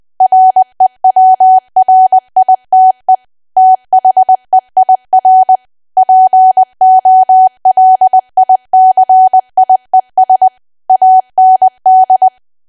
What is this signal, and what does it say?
Morse code, heard as a keyed tone, about 20 wpm, 750 Hz. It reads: REWRITE THEIR POLICIES AND
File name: cwa / W1AW-Level3 / 20WPM